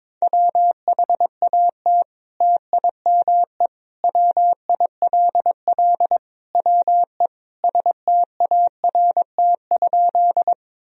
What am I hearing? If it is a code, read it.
Morse code, heard as a keyed tone, about 22 wpm, 700 Hz. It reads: WHAT TIME WILL WE START?
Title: Sentences from Top 1000 Words - Review Part 1 - 22wpm